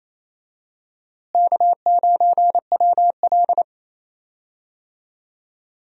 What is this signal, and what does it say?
Morse code, heard as a keyed tone, about 28 wpm, 700 Hz. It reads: K9WL
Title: USA Call Signs - 4 Characters - Repeated 2x - 28wpm